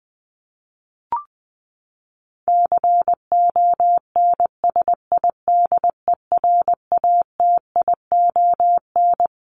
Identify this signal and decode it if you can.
Morse code, heard as a keyed tone, about 20 wpm, 700 Hz. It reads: CONSIDERATION
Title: Top 1251-1500 Words 20wpm